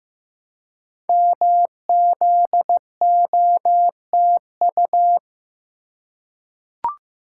Morse code, heard as a keyed tone, about 15 wpm, 700 Hz. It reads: MZOTU
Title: Code Groups - 5 Letters - 15wpm